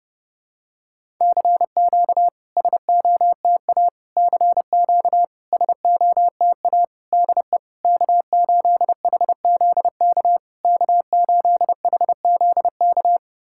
Morse code, heard as a keyed tone, about 30 wpm, 700 Hz. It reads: CQ SOTA CQ SOTA DE K8HZK K8HZK
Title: SOTA QSO Practice 30wpm